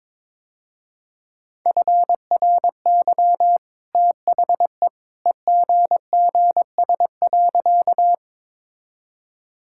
Morse code, heard as a keyed tone, about 22 wpm, 700 Hz. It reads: FRY THE EGGS.
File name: practice / 22wpm / Easy Sentences with Hints - 22wpm